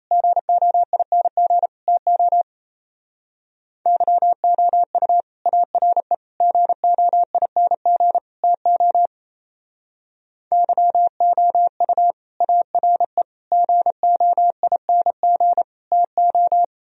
Morse code, heard as a keyed tone, about 35 wpm, 700 Hz. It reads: GOING TO YOU ARE GOING TO YOU ARE GOING TO
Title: Speed-Racing - Sets of 4 Words 28wpm